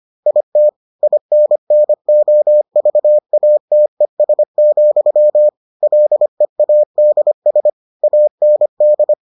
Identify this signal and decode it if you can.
Morse code, heard as a keyed tone, about 25 wpm, 600 Hz. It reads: IT INNOVATES, LEADS AND